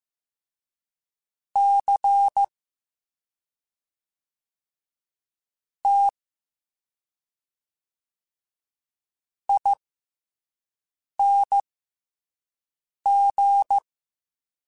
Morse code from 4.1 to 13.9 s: T ING

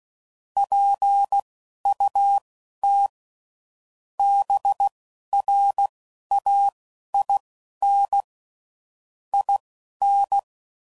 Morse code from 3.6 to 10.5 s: BRAIN IN

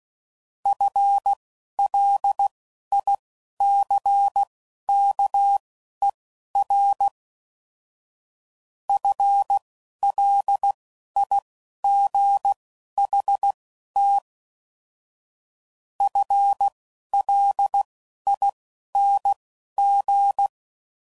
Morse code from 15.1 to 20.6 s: FLING

